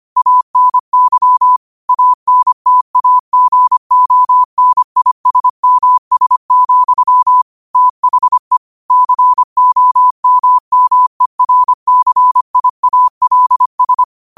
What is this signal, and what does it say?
Morse code, heard as a keyed tone, about 25 wpm, 1000 Hz. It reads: ANY ANTAGONISMS, THE COMMERCIALS